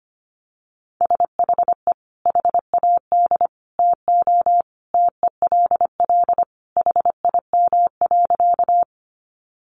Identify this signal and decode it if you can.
Morse code, heard as a keyed tone, about 25 wpm, 700 Hz. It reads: SHE HAD TO TELL HIM.
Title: Sentences from Top 200 Words 25wpm